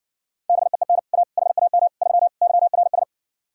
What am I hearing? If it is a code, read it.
Morse code, heard as a keyed tone, about 60 wpm, 700 Hz. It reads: BEEN A HAM 3 YRS